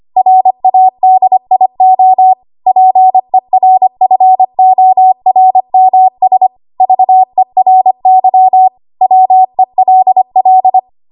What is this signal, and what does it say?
Morse code, heard as a keyed tone, about 25 wpm, 750 Hz. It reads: RADIO PERFORMS VERY WELL
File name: Archive / 25 WPM / 25WPM